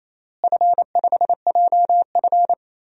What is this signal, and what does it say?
Morse code, heard as a keyed tone, about 28 wpm, 700 Hz. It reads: F5JF
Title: DXCC-100 Call Signs - 4 Characters - Repeated 3x - 28wpm